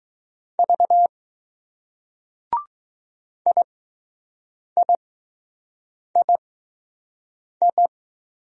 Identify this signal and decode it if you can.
Morse code, heard as a keyed tone, about 21 wpm, spaced as for 6 wpm, 700 Hz. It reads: V IIII